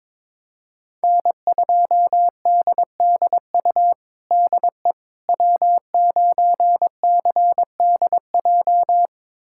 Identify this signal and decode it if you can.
Morse code, heard as a keyed tone, about 22 wpm, 700 Hz. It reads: N2DDU DE W9CDJ
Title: USA Callsign DE Callsign - 5 Characters - 22wpm